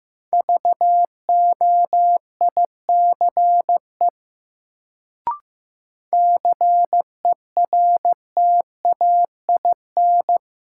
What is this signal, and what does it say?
Morse code, heard as a keyed tone, about 15 wpm, 700 Hz. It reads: VOICE CERTAIN
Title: Sentences from Top 500 Words - New Word Drill - 15wpm